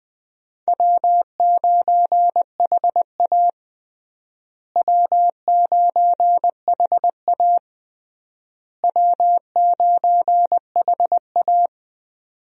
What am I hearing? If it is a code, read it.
Morse code, heard as a keyed tone, about 20 wpm, 700 Hz. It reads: W9HA W9HA W9HA
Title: USA Call Signs - 4 Characters - Repeated 3x - 20wpm